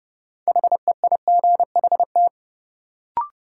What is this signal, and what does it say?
Morse code, heard as a keyed tone, about 30 wpm, 700 Hz. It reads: HEIGHT
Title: Sentences from Top 1600 Words - New Word Drill Alt - 30wpm